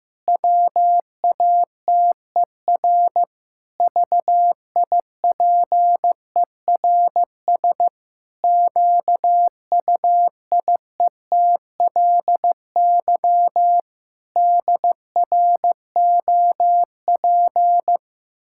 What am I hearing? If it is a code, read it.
Morse code, heard as a keyed tone, about 15 wpm, 700 Hz. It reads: WATER VIPERS QUIETLY DROP
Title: Pangram Challenge v2 - 15wpm